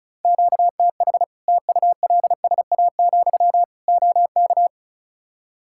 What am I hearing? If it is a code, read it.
Morse code, heard as a keyed tone, about 36 wpm, 700 Hz. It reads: QTH TULSA, OK
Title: Practice QSO - Exchange 1 - Beginning - 35wpm